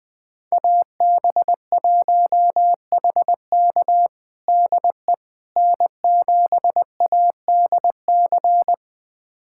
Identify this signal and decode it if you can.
Morse code, heard as a keyed tone, about 20 wpm, 700 Hz. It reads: AB1HK DE N7ADC
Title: USA Callsign DE Callsign - 5 Characters - 20wpm